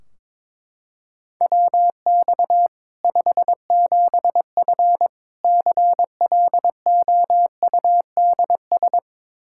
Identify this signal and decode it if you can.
Morse code, heard as a keyed tone, about 23 wpm, 700 Hz. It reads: WX 57F CLOUDS
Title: K1USN SST QSO Practice 22wpm